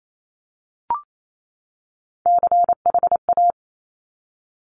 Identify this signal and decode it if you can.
Morse code, heard as a keyed tone, about 28 wpm, 700 Hz. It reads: CHA
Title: Top Three Letter Combinations in Words 28wpm